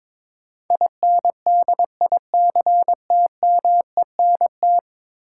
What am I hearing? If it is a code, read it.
Morse code, heard as a keyed tone, about 22 wpm, 700 Hz. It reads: INDICTMENT
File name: practice / 22wpm / 10 Letter Words - First 3 Letters Given - 22wpm